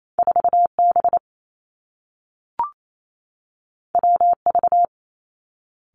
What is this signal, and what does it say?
Morse code, heard as a keyed tone, about 28 wpm, 700 Hz. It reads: 4 B W V
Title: Two Challenging Letter-Number Combinations 28wpm